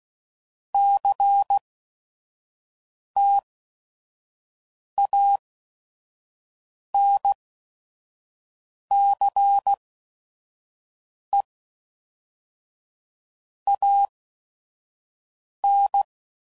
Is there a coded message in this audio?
Yes: Morse code from 0.7 to 16.0 s, heard as a keyed tone, about 16 wpm, 800 Hz. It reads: C T A N C E A N